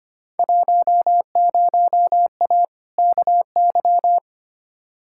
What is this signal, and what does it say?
Morse code, heard as a keyed tone, about 25 wpm, 700 Hz. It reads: 10A KY
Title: Field Day Exchanges Encore - 25wpm